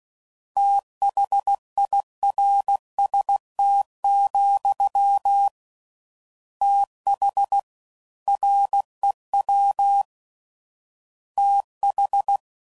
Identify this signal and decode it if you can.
Morse code, heard as a keyed tone, about 16 wpm, 800 Hz. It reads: THIRST, TH REW TH
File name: Prefixes / FLs & THs / Fast TH